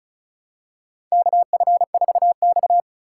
Morse code, heard as a keyed tone, about 36 wpm, 700 Hz. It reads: KF4X
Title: USA Call Signs - 4 Characters - 3x Character Spacing - 35wpm